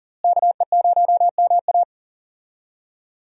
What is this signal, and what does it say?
Morse code, heard as a keyed tone, about 38 wpm, 700 Hz. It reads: KE0MA
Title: USA Call Signs - 5 Characters - Repeated 3x - 40wpm